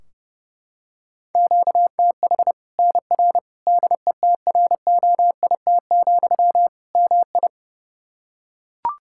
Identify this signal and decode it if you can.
Morse code, heard as a keyed tone, about 30 wpm, 700 Hz. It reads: QTH NR DETROIT, MI
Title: Practice QSO - Exchange 1 - Beginning - 30wpm